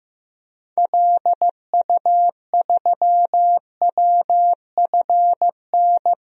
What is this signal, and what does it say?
Morse code, heard as a keyed tone, about 15 wpm, 700 Hz. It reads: LU3WFN
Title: DXCC-100 Call Signs - 6 Characters - 15wpm